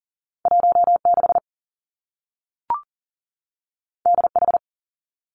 Morse code, heard as a keyed tone, about 39 wpm, 700 Hz. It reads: 16 DH
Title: Two Challenging Letter-Number Combinations 40wpm